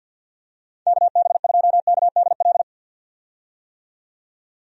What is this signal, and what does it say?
Morse code, heard as a keyed tone, about 49 wpm, 700 Hz. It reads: KB2KDL
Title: USA Call Signs - 6 Characters - 50wpm